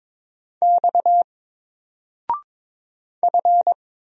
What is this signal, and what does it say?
Morse code, heard as a keyed tone, about 22 wpm, 700 Hz. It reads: X F
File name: Single-Letter-Number-22wpm